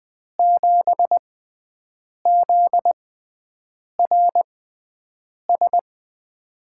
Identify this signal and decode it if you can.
Morse code, heard as a keyed tone, about 20 wpm, spaced as for 7 wpm, 700 Hz. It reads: 7ZRS
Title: USA Call Signs - 6 Characters - 6x Character Spacing - 20wpm